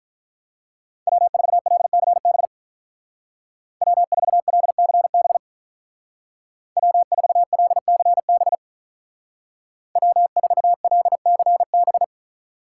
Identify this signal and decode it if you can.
Morse code, heard as a keyed tone, about 43 wpm, 700 Hz. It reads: W4LCB W4LCB W4LCB W4LCB
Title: Speed Racing - USA 1x3 Call Signs 35wpm